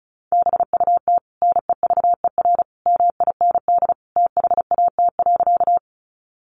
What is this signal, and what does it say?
Morse code, heard as a keyed tone, about 34 wpm, 700 Hz. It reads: BUT NEVER MIND THAT.
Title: Sentences from Top 500 Words 35wpm